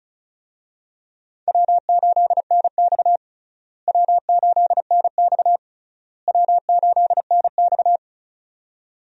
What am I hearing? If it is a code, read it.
Morse code, heard as a keyed tone, about 36 wpm, 700 Hz. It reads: W8NX W8NX W8NX
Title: USA Call Signs - 4 Characters - Repeated 3x - 35wpm